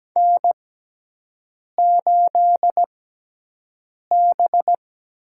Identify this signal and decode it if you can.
Morse code, heard as a keyed tone, about 17 wpm, spaced as for 6 wpm, 700 Hz. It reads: N8B
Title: USA Call Signs - 4 Characters - 6x Character Spacing - 17wpm